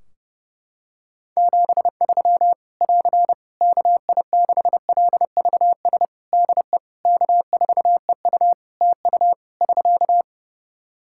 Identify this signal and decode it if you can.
Morse code, heard as a keyed tone, about 30 wpm, 700 Hz. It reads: 73 <AR> KI6LVS DE K4EU TU <SK>
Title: K1USN SST QSO Practice 30wpm